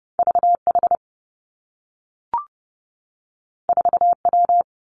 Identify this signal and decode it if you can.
Morse code, heard as a keyed tone, about 30 wpm, 700 Hz. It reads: V H 4 W
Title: Two Challenging Letter-Number Combinations 30wpm